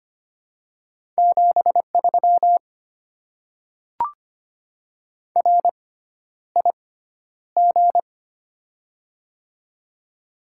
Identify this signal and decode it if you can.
Morse code, heard as a keyed tone, about 25 wpm, spaced as for 11 wpm, 700 Hz. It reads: 73 RIG
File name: Top 20 QSO Elements - 6x Character Spacing - 25wpm